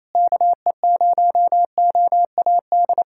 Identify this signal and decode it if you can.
Morse code, heard as a keyed tone, about 27 wpm, 700 Hz. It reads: KE0OAD
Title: USA Callsign DE Callsign - 6 Characters - 28wpm